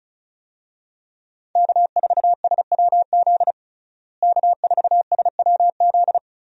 Morse code, heard as a keyed tone, about 36 wpm, 700 Hz. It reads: K4SWZ K4SWZ
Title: USA Call Signs - 5 Characters - Repeated 2x - 35wpm